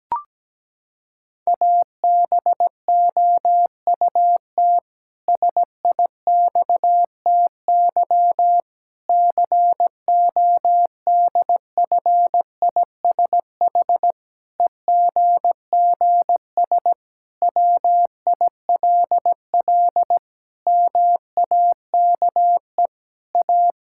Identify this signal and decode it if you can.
Morse code, heard as a keyed tone, about 17 wpm, 700 Hz. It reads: ABOUT SIXTY CODFISH EGGS WILL MAKE A
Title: Pangram Challenge v2 - 17wpm